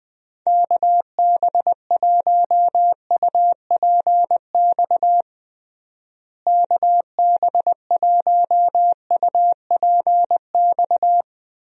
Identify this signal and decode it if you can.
Morse code, heard as a keyed tone, about 20 wpm, 700 Hz. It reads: KB1UPX KB1UPX
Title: USA Call Signs - 6 Characters - Repeated 3x - 20wpm